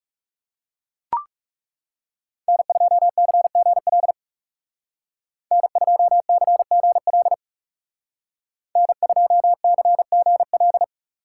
Morse code, heard as a keyed tone, about 39 wpm, 700 Hz. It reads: N2CGL N2CGL N2CGL